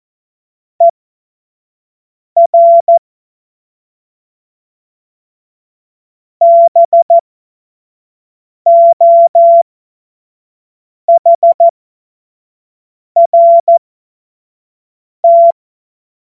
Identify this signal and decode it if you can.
Morse code, heard as a keyed tone, about 14 wpm, spaced as for 5 wpm, 680 Hz. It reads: ER BOHRT
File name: Morsekurs / Witze / Witz 02. November 2021, 25 F70